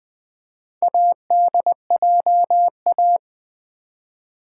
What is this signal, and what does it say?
Morse code, heard as a keyed tone, about 20 wpm, 700 Hz. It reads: ADJA
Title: Build A Word - 2 Letters At A Time - 8-Letter Words - 20wpm